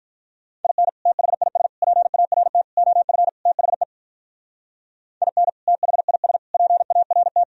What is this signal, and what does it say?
Morse code, heard as a keyed tone, about 44 wpm, 700 Hz. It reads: IN THIS PART OF THE IN THIS PART